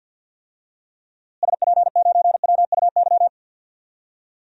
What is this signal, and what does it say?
Morse code, heard as a keyed tone, about 50 wpm, 700 Hz. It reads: SP9WUY
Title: DXCC-100 Call Signs - 6 Characters - Repeated 2x - 50wpm